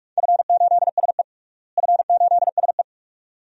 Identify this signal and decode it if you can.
Morse code, heard as a keyed tone, about 46 wpm, 700 Hz. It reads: F8SE F8SE
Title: DXCC-100 Call Signs - 4 Characters - Repeated 2x - 45wpm